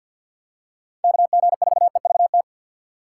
Morse code, heard as a keyed tone, about 52 wpm, 700 Hz. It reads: KG4EVT
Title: USA Call Signs - 6x Character Spacing - 50wpm